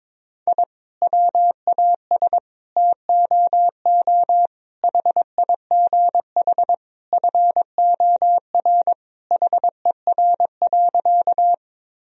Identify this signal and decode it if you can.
Morse code, heard as a keyed tone, about 22 wpm, 700 Hz. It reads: I WAS TOO HIGH FOR HER.
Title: Sentences from Top 1000 Words - Review Part 1 - 22wpm